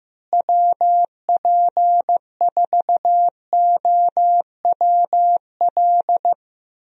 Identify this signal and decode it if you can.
Morse code, heard as a keyed tone, about 15 wpm, 700 Hz. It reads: WP4OWL
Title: USA Call Signs - 6 Characters - Repeated 3x - 15wpm